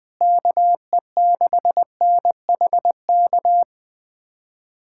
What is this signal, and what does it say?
Morse code, heard as a keyed tone, about 20 wpm, 700 Hz. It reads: KE6NHK
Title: USA Callsign DE Callsign 20wpm